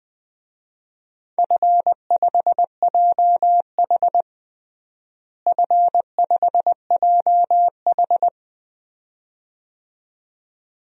Morse code, heard as a keyed tone, about 20 wpm, 700 Hz. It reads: F5JH F5JH
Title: DXCC-100 Call Signs - 4 Characters - Repeated 2x - 20wpm